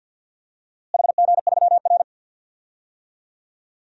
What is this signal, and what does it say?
Morse code, heard as a keyed tone, about 51 wpm, 700 Hz. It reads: HG3R
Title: DXCC-100 Call Signs - 4 Characters - Repeated 2x - 50wpm